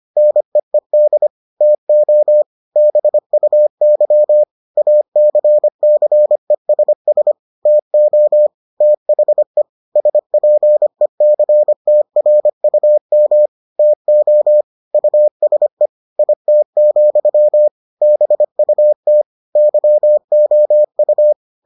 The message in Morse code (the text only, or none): NEED TO BUY ACCESS TO THE SPECTRUM TO USE IT, BUT YOU